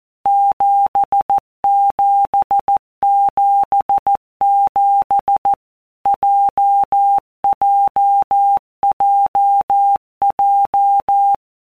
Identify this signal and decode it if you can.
Morse code, heard as a keyed tone, about 14 wpm, 800 Hz. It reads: 7777 JJJJ
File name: Basic Letter Groups By WPM's / Groups@13 / JZ17@13